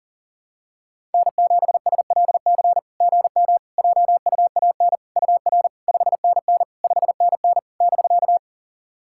Morse code, heard as a keyed tone, about 40 wpm, 690 Hz. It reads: N7SLC GM JUAN UR 5NN 5NN <BK>
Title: SOTA QSO Practice 40wpm